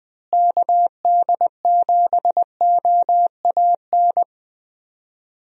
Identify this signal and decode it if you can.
Morse code, heard as a keyed tone, about 20 wpm, 700 Hz. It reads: KD7OAN
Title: USA Call Signs - 6 Characters - Repeated 3x - 20wpm